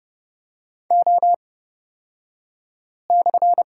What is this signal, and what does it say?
Morse code, heard as a keyed tone, about 30 wpm, spaced as for 5 wpm, 700 Hz. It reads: O/
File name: Lesson 38d - Letter q - Two Characters 30wpm